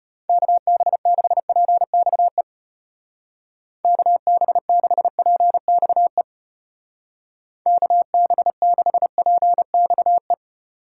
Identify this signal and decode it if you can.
Morse code, heard as a keyed tone, about 34 wpm, 700 Hz. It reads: KB6PXE KB6PXE KB6PXE